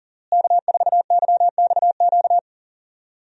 Morse code, heard as a keyed tone, about 39 wpm, 700 Hz. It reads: K4YXQ
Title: USA Call Signs - 5 Characters - Repeated 3x - 40wpm